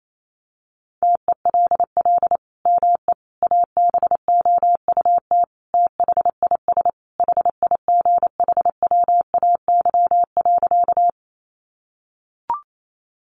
Morse code, heard as a keyed tone, about 28 wpm, 700 Hz. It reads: TELL ME ABOUT THIS HIGHWAY.
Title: Sentences from Top 1400 Words 28wpm